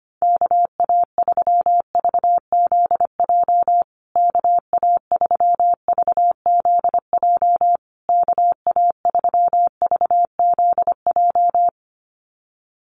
Morse code, heard as a keyed tone, about 25 wpm, 700 Hz. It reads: KA3VZJ KA3VZJ KA3VZJ
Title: USA Call Signs Repeated 3x 25wpm